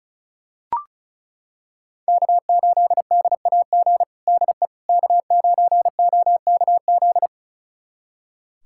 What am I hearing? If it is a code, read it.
Morse code, heard as a keyed tone, about 36 wpm, 700 Hz. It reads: K8DAG DE K9OKZ
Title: USA Callsign DE Callsign - 5 Characters - 35wpm